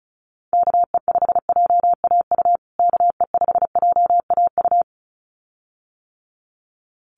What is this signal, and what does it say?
Morse code, heard as a keyed tone, about 34 wpm, 700 Hz. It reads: KE5JAU KE5JAU
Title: USA Call Signs Repeated 2x 35wpm